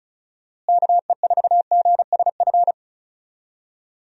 Morse code, heard as a keyed tone, about 34 wpm, 700 Hz. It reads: KE4GSF
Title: USA Call Signs - 6 Characters - Repeated 2x - 35wpm